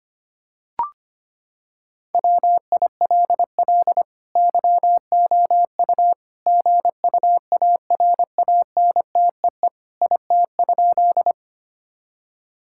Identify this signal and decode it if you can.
Morse code, heard as a keyed tone, about 25 wpm, 700 Hz. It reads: WILL YOU GUARANTEE IT?
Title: Sentences from Top 1600 Words 25wpm